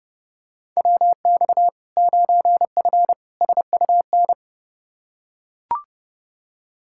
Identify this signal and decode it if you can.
Morse code, heard as a keyed tone, about 30 wpm, 700 Hz. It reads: WX 9F SUN
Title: Practice QSO - Exchange 2 - Info - 30wpm